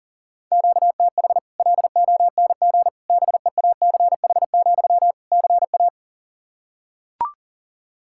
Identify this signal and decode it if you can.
Morse code, heard as a keyed tone, about 40 wpm, 700 Hz. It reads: QTH LONG BEACH, CA
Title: Practice QSO - Exchange 1 - Beginning - 40wpm